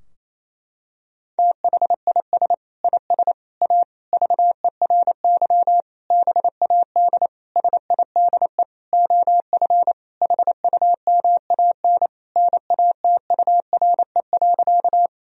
Morse code, heard as a keyed tone, about 28 wpm, 700 Hz. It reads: THIS IS A VERY BAD SIDE OF HUMAN NATURE.
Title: Sentences from Top 1000 Words - Review Part 3 - 28wpm